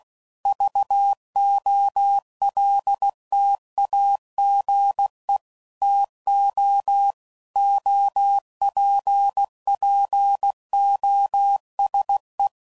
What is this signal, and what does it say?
Morse code, heard as a keyed tone, about 16 wpm, 800 Hz. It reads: VOLTAGE TO OPPOSE
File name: CW Code Groups @15WPM / Technical Subjects / XC